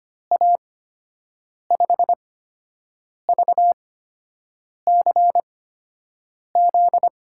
Morse code, heard as a keyed tone, about 25 wpm, spaced as for 7 wpm, 700 Hz. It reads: A5VCZ
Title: USA Call Signs - 6 Characters - 8x Character Spacing - 25wpm